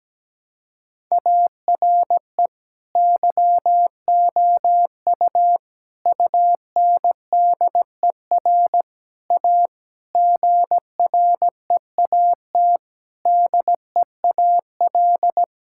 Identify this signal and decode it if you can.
Morse code, heard as a keyed tone, about 17 wpm, 700 Hz. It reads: ARE YOU UNDER A GREAT DEAL
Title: Sentences from Top 1000 Words - Review Part 6 - 17wpm